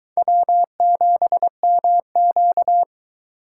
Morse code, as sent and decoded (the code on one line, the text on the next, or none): .-- --... -- --.-
W7MQ